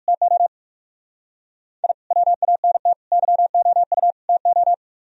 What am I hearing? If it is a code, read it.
Morse code, heard as a keyed tone, about 52 wpm, 700 Hz. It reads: TO I WANT YOU TO
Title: Speed-Racing - Sets of 4 Words 45wpm